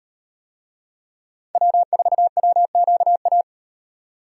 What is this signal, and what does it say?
Morse code, heard as a keyed tone, about 38 wpm, 700 Hz. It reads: W4WQA